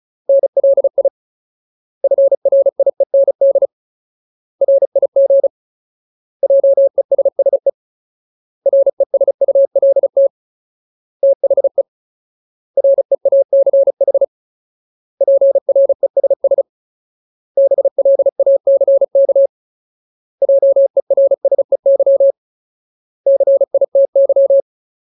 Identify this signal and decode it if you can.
Morse code, heard as a keyed tone, about 34 wpm, 550 Hz. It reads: NLI FRIEND RIG JESSE RESULT THE REACH PRESS BLACK JERSEY CITY
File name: Ultimate Word List / 35WPM Ultimate